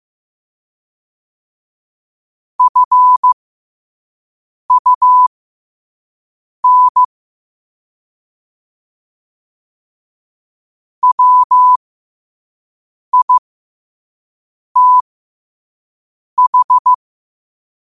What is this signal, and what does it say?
Morse code, heard as a keyed tone, about 15 wpm, spaced as for 5 wpm, 1000 Hz. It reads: FUN WITH
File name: qotdfiles / 5wpm